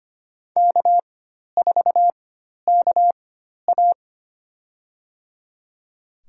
Morse code, heard as a keyed tone, about 25 wpm, spaced as for 12 wpm, 690 Hz. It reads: K4KA
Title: USA Call Signs - 4 Characters - 4x Character Spacing - 25wpm